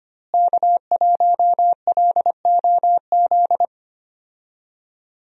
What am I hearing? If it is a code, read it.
Morse code, heard as a keyed tone, about 25 wpm, 700 Hz. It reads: K1LOZ
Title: USA Call Signs - 5 Characters - Repeated 3x - 25wpm